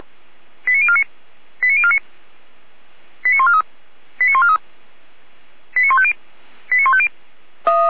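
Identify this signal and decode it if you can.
Five-tone selective call (ZVEI), heard as quick runs of five tones, tones of about 70 ms. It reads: 89049 89049 89143 89143 89150 89150
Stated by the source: ZVEI five-tone selective call received on SDR, from feuerwehr-orlamuende fire brigade server.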